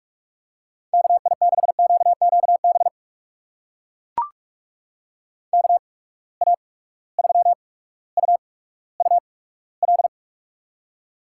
Morse code, heard as a keyed tone, about 44 wpm, spaced as for 19 wpm, 700 Hz. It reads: KI6QQB KA3UUL
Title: USA Call Signs - 8x Character Spacing - 45wpm